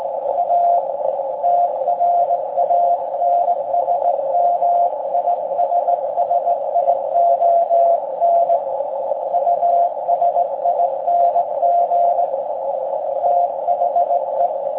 Morse code, heard as a keyed tone, about 19 wpm, 690 Hz, with noise. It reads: A TRANSMISSION USING THE